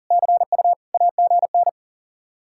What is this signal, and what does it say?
Morse code, heard as a keyed tone, about 50 wpm, 700 Hz. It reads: CU AGN
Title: Speed-Racing Common QSO Phrases - 40wpm